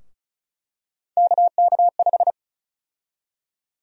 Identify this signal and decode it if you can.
Morse code, heard as a keyed tone, about 34 wpm, 700 Hz. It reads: KK5
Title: USA Call Sign Prefixes - 35wpm